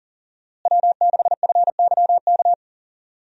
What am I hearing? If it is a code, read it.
Morse code, heard as a keyed tone, about 40 wpm, 700 Hz. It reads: W6FYK
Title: USA Call Signs - 5 Characters - Repeated 2x - 40wpm